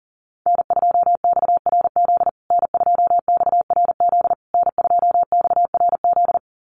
Morse code, heard as a keyed tone, about 37 wpm, 700 Hz. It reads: N2XRZ N2XRZ N2XRZ
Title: USA Call Signs Repeated 3x 40wpm